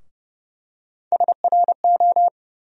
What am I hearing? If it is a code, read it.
Morse code, heard as a keyed tone, about 30 wpm, 700 Hz. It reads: SRO
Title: USA Call Sign Postfixes - 3 Letters 30wpm